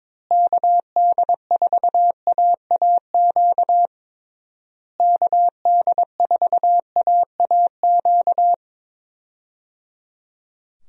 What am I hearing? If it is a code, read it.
Morse code, heard as a keyed tone, about 22 wpm, 700 Hz. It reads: KD4AAQ KD4AAQ
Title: USA Call Signs - 6 Characters - Repeated 3x - 22wpm